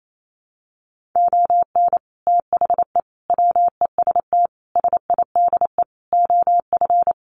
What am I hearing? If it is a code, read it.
Morse code, heard as a keyed tone, about 28 wpm, 700 Hz. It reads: ON THE WEST SIDE OF
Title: Sets of 5 Words Encore2 28wpm